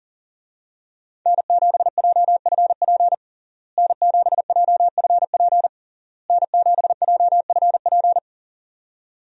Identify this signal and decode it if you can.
Morse code, heard as a keyed tone, about 40 wpm, 700 Hz. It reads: N7JFP N7JFP N7JFP
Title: USA Call Signs - 5 Characters - Repeated 3x - 40wpm